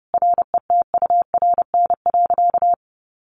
Morse code, heard as a keyed tone, about 30 wpm, 700 Hz. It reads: RETURN.
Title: A Sound Of Thunder 30wpm